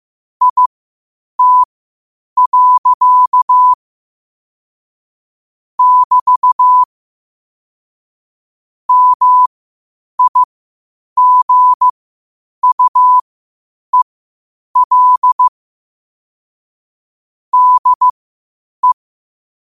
Morse code, heard as a keyed tone, about 15 wpm, spaced as for 8 wpm, 1000 Hz. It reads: IT. <BT> MIGUEL DE
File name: qotdfiles / 8wpm